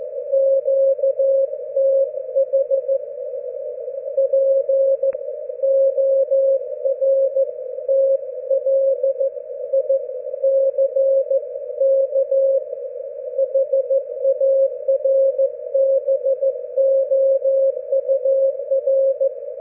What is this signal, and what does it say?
Morse code, heard as a keyed tone, about 14 wpm, 540 Hz. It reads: QTH PORTLICK HARBOUR
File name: GW3UEP / EI0CF P PortlickHarbour 19June12